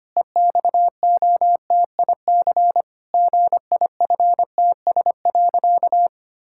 Morse code, heard as a keyed tone, about 25 wpm, 700 Hz. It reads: EXOTIC GIFTS.
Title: Pangram Challenge v2 - 25wpm